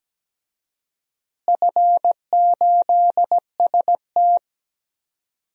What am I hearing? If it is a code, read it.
Morse code, heard as a keyed tone, about 17 wpm, 700 Hz. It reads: F8ST
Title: DXCC-100 Call Signs - 4 Characters - Repeated 3x - 17wpm